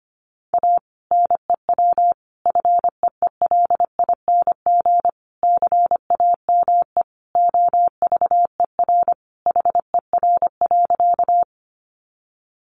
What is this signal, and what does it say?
Morse code, heard as a keyed tone, about 25 wpm, 700 Hz. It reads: A NEW FEELING CAME OVER HER.
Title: Sentences from Top 1200 Words 25wpm